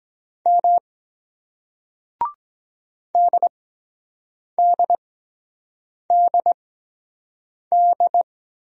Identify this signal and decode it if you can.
Morse code, heard as a keyed tone, about 22 wpm, spaced as for 7 wpm, 700 Hz. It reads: M DDDD